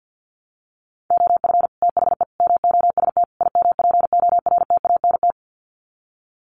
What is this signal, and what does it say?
Morse code, heard as a keyed tone, about 50 wpm, 700 Hz. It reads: OF THE MOST IMPORTANT
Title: Sets of 4 Words 50wpm